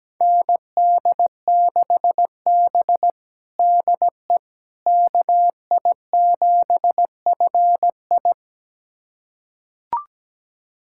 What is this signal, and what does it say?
Morse code, heard as a keyed tone, about 17 wpm, 700 Hz. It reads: ND6B DE KI7FI
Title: USA Callsign DE Callsign 17wpm